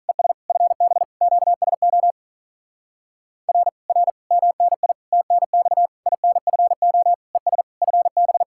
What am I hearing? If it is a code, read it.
Morse code, heard as a keyed tone, about 39 wpm, 700 Hz. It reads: ES FB QSO R R MNI TNX INFO ES FB
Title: Speed-Racing Common QSO Phrases - 35wpm